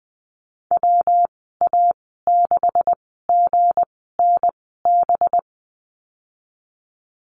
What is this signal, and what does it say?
Morse code, heard as a keyed tone, about 20 wpm, spaced as for 15 wpm, 700 Hz. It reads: WA6GNB
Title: 186 Lesson 30g - Letter b - Call Signs 20wpm